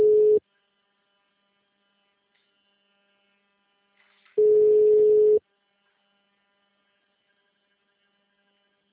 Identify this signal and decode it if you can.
Ringback tone (Europe, 425 Hz).